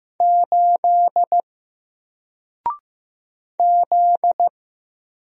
Morse code, heard as a keyed tone, about 15 wpm, 700 Hz. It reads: 8 Z
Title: Single Letter-Number 15wpm